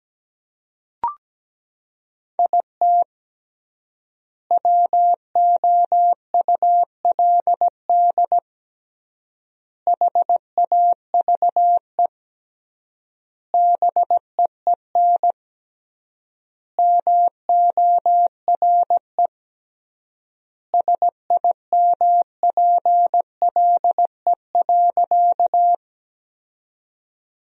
Morse code, heard as a keyed tone, about 17 wpm, 700 Hz. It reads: IT WOULD HAVE BEEN MORE SIMPLE.
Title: 3x Word-Spacing - Sentences from Top 400 Words - 17wpm